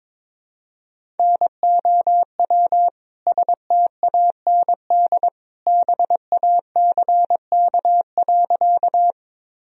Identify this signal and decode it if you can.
Morse code, heard as a keyed tone, about 22 wpm, 700 Hz. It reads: NOW STAND BACK.
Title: Sentences from Top 1000 Words - Review Part 1 - 22wpm